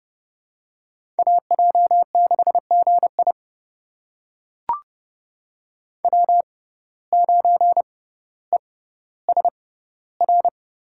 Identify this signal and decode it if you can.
Morse code, heard as a keyed tone, about 31 wpm, spaced as for 16 wpm, 700 Hz. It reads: AJ6GI W9ESR
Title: USA Call Signs - 5 Characters - 6x Character Spacing - 30wpm